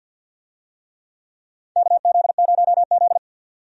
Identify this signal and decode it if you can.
Morse code, heard as a keyed tone, about 52 wpm, 700 Hz. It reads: KZ0Z
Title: USA Call Signs - 4 Characters - 50wpm